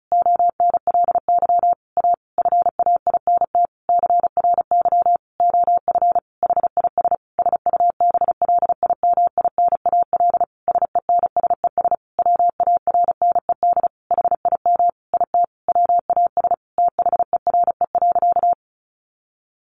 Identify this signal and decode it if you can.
Morse code, heard as a keyed tone, about 34 wpm, 700 Hz. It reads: ONLY A FAINT CRY OF HIS SUBLIMINAL SENSES WARNED HIM IT WAS THERE.
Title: A Sound Of Thunder 35wpm